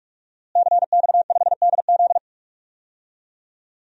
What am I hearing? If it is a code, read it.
Morse code, heard as a keyed tone, about 45 wpm, 700 Hz. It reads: CX5DZ
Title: DXCC-100 Call Signs - 5 Characters - Repeated 3x - 45wpm